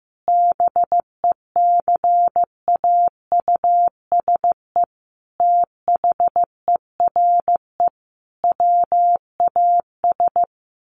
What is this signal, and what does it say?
Morse code, heard as a keyed tone, about 15 wpm, 700 Hz. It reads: BECAUSE THERE WAS
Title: Sentences from Top 1000 Words 15wpm